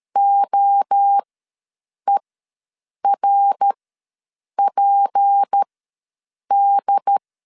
Morse code, heard as a keyed tone, about 13 wpm, spaced as for 7 wpm, 800 Hz. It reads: OERPD
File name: LEZIONI / 7 GF9Z / 7 G ws 12wpm 3140